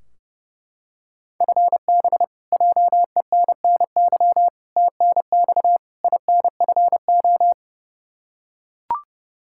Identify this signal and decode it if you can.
Morse code, heard as a keyed tone, about 30 wpm, 700 Hz. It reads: FB JENNY TNX INFO
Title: K1USN SST QSO Practice 30wpm